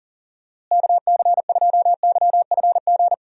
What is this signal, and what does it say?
Morse code, heard as a keyed tone, about 40 wpm, 700 Hz. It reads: KC2YFG
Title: USA Call Signs - 6 Characters - Repeated 2x - 40wpm